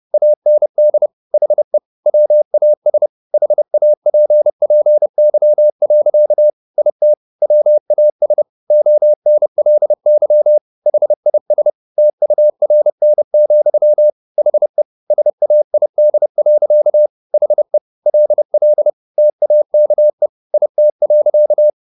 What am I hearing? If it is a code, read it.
Morse code, heard as a keyed tone, about 30 wpm, 600 Hz. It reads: AND HE WAS HAPPY. IT WAS ONLY HIS TURN, HE SAID. HE LL TAKE IT.
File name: the old man and the sea / the old man and the sea 1 30wpm